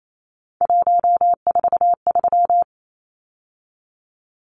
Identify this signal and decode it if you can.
Morse code, heard as a keyed tone, about 28 wpm, 700 Hz. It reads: 143
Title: Numbers - 3-Digit 28wpm